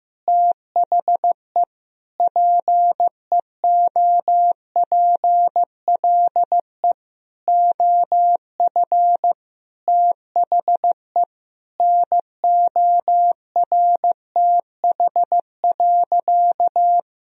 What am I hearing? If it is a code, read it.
Morse code, heard as a keyed tone, about 15 wpm, 700 Hz. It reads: THE PEOPLE OF THE NORTH.
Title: Sentences from Top 1000 Words - Review Part 1 - 15wpm